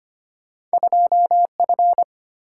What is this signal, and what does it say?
Morse code, heard as a keyed tone, about 32 wpm, 700 Hz. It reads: 2F